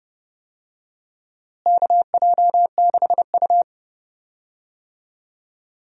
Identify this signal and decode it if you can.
Morse code, heard as a keyed tone, about 30 wpm, 700 Hz. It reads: KJ6U
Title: USA Call Signs - 4 Characters - Repeated 2x - 30wpm